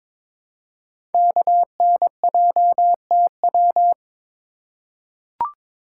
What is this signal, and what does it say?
Morse code, heard as a keyed tone, about 22 wpm, 700 Hz. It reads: KNJTW
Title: Code Groups - 5 Letter:Numbers - 22wpm